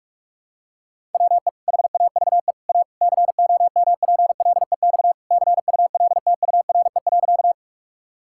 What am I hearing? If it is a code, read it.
Morse code, heard as a keyed tone, about 46 wpm, 700 Hz. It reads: WE HAVE A COMPLEX CULTURE.
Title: Sentences from Top 1000 Words - Review Part 6 - 45wpm